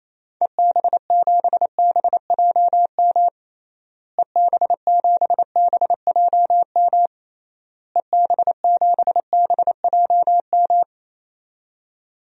Morse code, heard as a keyed tone, about 28 wpm, 700 Hz. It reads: EB7BJM EB7BJM EB7BJM
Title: DXCC-100 Call Signs - 6 Characters - Repeated 3x - 28wpm